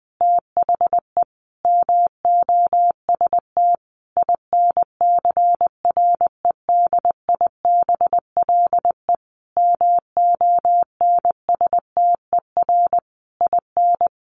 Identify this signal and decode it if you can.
Morse code, heard as a keyed tone, about 20 wpm, 700 Hz. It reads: THE MOST INCREDIBLE MONSTER IN
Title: A Sound Of Thunder 20wpm